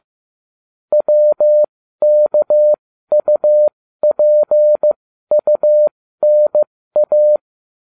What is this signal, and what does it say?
Morse code, heard as a keyed tone, about 15 wpm, spaced as for 13 wpm, 600 Hz. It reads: WKUPUNA